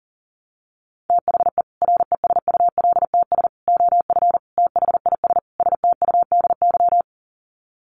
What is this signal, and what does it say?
Morse code, heard as a keyed tone, about 40 wpm, 700 Hz. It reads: THE RESULTS OF THIS STUDY
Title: Sets of 5 Words 40wpm